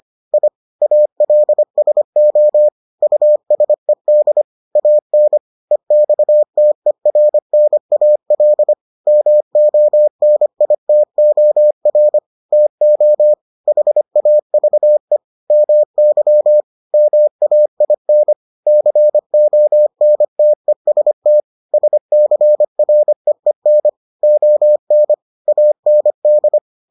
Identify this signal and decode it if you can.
Morse code, heard as a keyed tone, about 25 wpm, 600 Hz. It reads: I ALSO USED AN EXTERNAL MONITOR TO HAVE MY MAIN CONTEST SCREEN ON AND